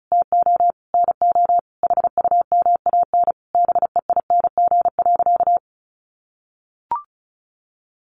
Morse code, heard as a keyed tone, about 34 wpm, 700 Hz. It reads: TO NO HUMAN BEING.
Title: Sentences from Top 1500 Words 35wpm